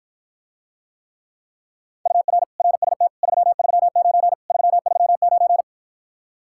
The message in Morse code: UR RST 339 339

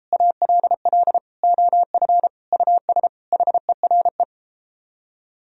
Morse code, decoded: ALL OF US HERE

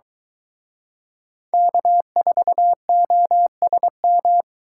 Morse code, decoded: K4OSM

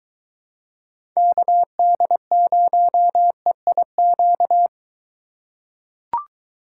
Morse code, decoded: KD0EIQ

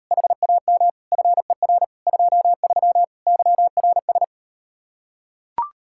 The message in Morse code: HAM FER 23 YRS

